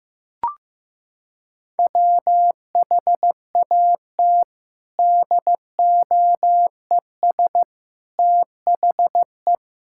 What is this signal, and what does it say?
Morse code, heard as a keyed tone, about 15 wpm, 700 Hz. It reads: WHAT DOES THE
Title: Sentences from Top 1000 Words - Review Part 6 - 15wpm